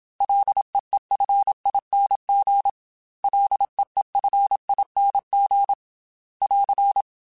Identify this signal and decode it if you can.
Morse code, heard as a keyed tone, about 27 wpm, 800 Hz. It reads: LEEFING LEEFING <AR>
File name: Ls / Single L m3s / Single L Prefixe 3x2 Twice @25